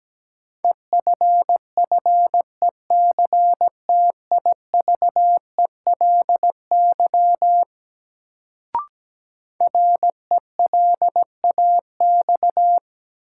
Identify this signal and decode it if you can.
Morse code, heard as a keyed tone, about 17 wpm, 700 Hz. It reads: EFFECTIVELY RELAX
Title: Sentences from Top 1400 Words - New Word Drill - 17wpm